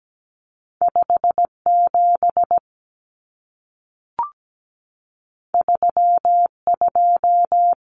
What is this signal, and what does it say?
Morse code, heard as a keyed tone, about 17 wpm, 700 Hz. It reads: 57 32
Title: Common Cut Numbers - 2-Digit 17wpm